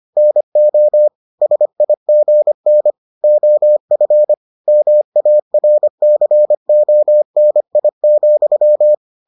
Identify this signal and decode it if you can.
Morse code, heard as a keyed tone, about 25 wpm, 600 Hz. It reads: NO SIGN OF MARCONI,